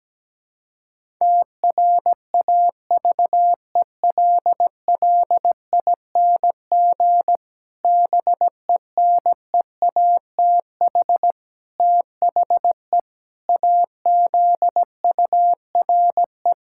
Morse code, heard as a keyed tone, about 17 wpm, 700 Hz. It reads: TRAVELLING BENEATH THE AZURE
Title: Pangram Challenge v2 - 17wpm